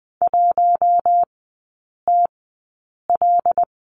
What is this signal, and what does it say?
Morse code, heard as a keyed tone, about 20 wpm, spaced as for 8 wpm, 700 Hz. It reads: 1TL
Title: 178 Lesson 29e - Slash - Three Characters 20wpm